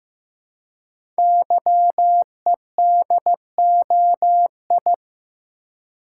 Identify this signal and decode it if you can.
Morse code, heard as a keyed tone, about 15 wpm, 700 Hz. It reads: YEDOI
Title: Code Groups - 5 Letters - 15wpm